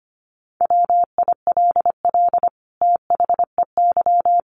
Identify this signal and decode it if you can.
Morse code, heard as a keyed tone, about 25 wpm, 700 Hz. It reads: WILL THEY
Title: Sentences from Top 1400 Words 25wpm